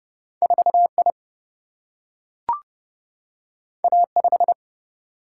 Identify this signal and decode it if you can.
Morse code, heard as a keyed tone, about 30 wpm, 700 Hz. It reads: 4I A5
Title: Single-Letter Single-Number 30wpm